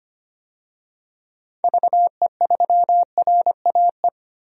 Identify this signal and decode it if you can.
Morse code, heard as a keyed tone, about 25 wpm, 700 Hz. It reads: VE3RAE
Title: DXCC-100 Call Signs - 6 Characters - Repeated 2x - 25wpm